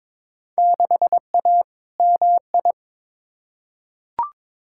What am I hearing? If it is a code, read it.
Morse code, heard as a keyed tone, about 22 wpm, 700 Hz. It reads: 6A MI
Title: Field Day Exchanges - 22wpm